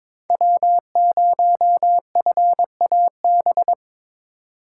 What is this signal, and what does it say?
Morse code, heard as a keyed tone, about 22 wpm, 700 Hz. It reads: W0FAB
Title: USA Call Signs - 5 Characters - 4x Character Spacing - 22wpm